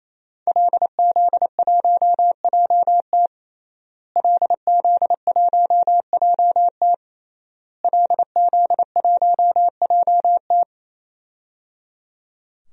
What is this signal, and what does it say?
Morse code, heard as a keyed tone, about 28 wpm, 700 Hz. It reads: LZ1JT LZ1JT LZ1JT
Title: DXCC-100 Call Signs - 5 Characters - Repeated 3x - 28wpm